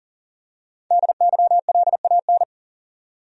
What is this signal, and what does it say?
Morse code, heard as a keyed tone, about 40 wpm, 700 Hz. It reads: DYLAN
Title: USA Common Names - Max 5 Letters - 40wpm